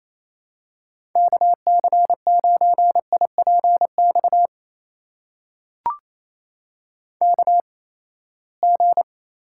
Morse code, heard as a keyed tone, about 28 wpm, spaced as for 19 wpm, 700 Hz. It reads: KC9IPX KG
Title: USA Call Signs - 6 Characters - 8x Character Spacing - 28wpm